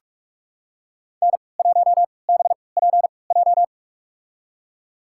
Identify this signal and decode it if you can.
Morse code, heard as a keyed tone, about 46 wpm, spaced as for 26 wpm, 700 Hz. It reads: N1BPJ
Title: USA Call Signs - 5 Characters - 3x Character Spacing - 45wpm